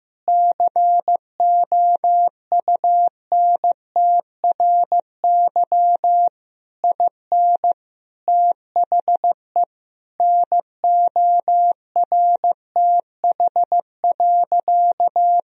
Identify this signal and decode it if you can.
Morse code, heard as a keyed tone, about 15 wpm, 700 Hz. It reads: COUNTRY IN THE NORTH.
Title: Sentences from Top 1000 Words - Review Part 1 - 15wpm